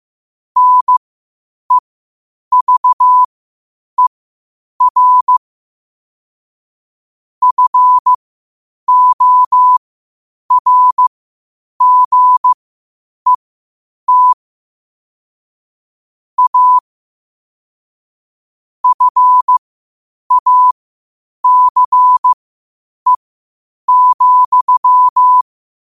Morse code, heard as a keyed tone, about 15 wpm, spaced as for 8 wpm, 1000 Hz. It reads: NEVER FORGET A FACE,
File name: qotdfiles / 8wpm